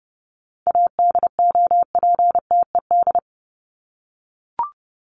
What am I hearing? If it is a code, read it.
Morse code, heard as a keyed tone, about 30 wpm, 700 Hz. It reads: ADOPTED
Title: ED Suffix - Top 100 Words with Base Words 30wpm